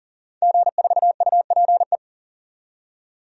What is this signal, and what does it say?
Morse code, heard as a keyed tone, about 40 wpm, 700 Hz. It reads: G4UPE
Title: DXCC-100 Call Signs - 5 Characters - 40wpm